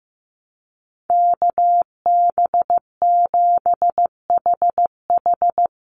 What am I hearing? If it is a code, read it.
Morse code, heard as a keyed tone, about 15 wpm, 700 Hz. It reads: KB7HH
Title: NA SOTA Chaser Callsigns 15wpm